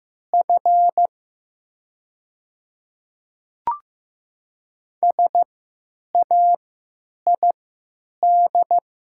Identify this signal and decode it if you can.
Morse code, heard as a keyed tone, about 15 wpm, spaced as for 9 wpm, 700 Hz. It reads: F SAID
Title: 3x Character Spacing - Top 300 Words - 15wpm